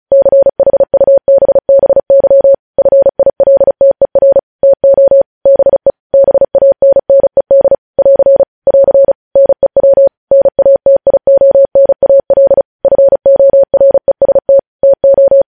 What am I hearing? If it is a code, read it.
Morse code, heard as a keyed tone, about 36 wpm, 550 Hz. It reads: CHUBBY FILTER TO BE BANNED <AR> <AR> NEW NATIONAL FOREST TO
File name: podcast / News headlines-1x-35x35